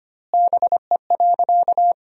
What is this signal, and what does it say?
Morse code, heard as a keyed tone, about 25 wpm, 700 Hz. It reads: BE.
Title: 2x Word-Spacing - Sentences from Top 300 Words - 25wpm